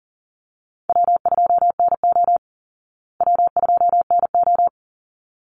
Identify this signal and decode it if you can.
Morse code, heard as a keyed tone, about 40 wpm, 700 Hz. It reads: W2NO W2NO
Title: Call Signs Repeated 3x - 40wpm